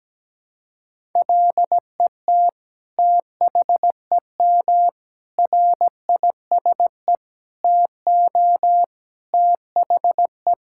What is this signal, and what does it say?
Morse code, heard as a keyed tone, about 17 wpm, 700 Hz. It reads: LET THEM RISE TO THE
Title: Sentences from Top 1000 Words - Review Part 6 - 17wpm